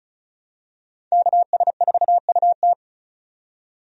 Morse code, heard as a keyed tone, about 34 wpm, 700 Hz. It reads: KS4UT
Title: USA Call Signs - 5 Characters - 35wpm